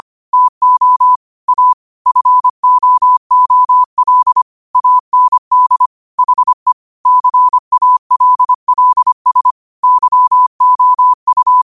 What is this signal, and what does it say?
Morse code, heard as a keyed tone, about 25 wpm, 1000 Hz. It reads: TO A FOOL AND HE CALLS YOU
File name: qotdfiles / 25wpm